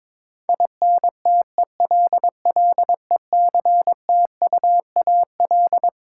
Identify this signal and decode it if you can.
Morse code, heard as a keyed tone, about 22 wpm, 700 Hz. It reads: INTELLECTUAL
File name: Twelve Letter Words - 22wpm